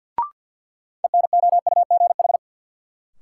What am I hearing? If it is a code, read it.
Morse code, heard as a keyed tone, about 49 wpm, 700 Hz. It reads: ENOUGH
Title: Sentences from Top 400 Words - New Word Drill - 50wpm